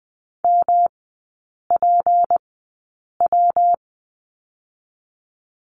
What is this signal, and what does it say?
Morse code, heard as a keyed tone, about 20 wpm, spaced as for 8 wpm, 690 Hz. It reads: MPW
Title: 172 Lesson 28e - Number 9 - Three Characters 20wpm